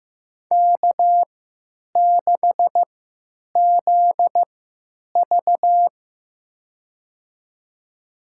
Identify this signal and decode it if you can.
Morse code, heard as a keyed tone, about 15 wpm, spaced as for 9 wpm, 700 Hz. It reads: K6ZV
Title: USA Call Signs - 4 Characters - 3x Character Spacing - 15wpm